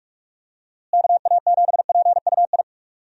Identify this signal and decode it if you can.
Morse code, heard as a keyed tone, about 44 wpm, 700 Hz. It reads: KA7PUI
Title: USA Call Signs - 6 Characters - Repeated 3x - 45wpm